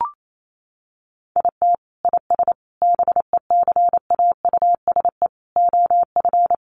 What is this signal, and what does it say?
Morse code, heard as a keyed tone, about 28 wpm, 700 Hz. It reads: IT IS BECAUSE OF
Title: Sentences from Top 600 Words 28wpm